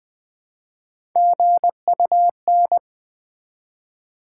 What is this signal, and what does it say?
Morse code, heard as a keyed tone, about 20 wpm, 700 Hz. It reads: GUN
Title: Build A Word - 1 Letter At A Time - 3-Letter Words - 20wpm